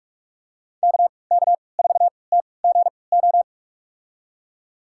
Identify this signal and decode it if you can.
Morse code, heard as a keyed tone, about 42 wpm, spaced as for 25 wpm, 700 Hz. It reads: KK4TGO
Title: USA Call Signs - 6 Characters - 3x Character Spacing - 45wpm